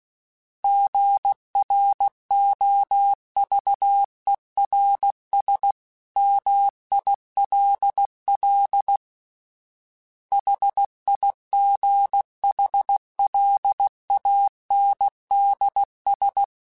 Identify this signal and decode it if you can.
Morse code, heard as a keyed tone, about 16 wpm, 800 Hz. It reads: GROVERS MILL HIGHLANDS
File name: fromSMs / FromSMs mp3s / Cities@15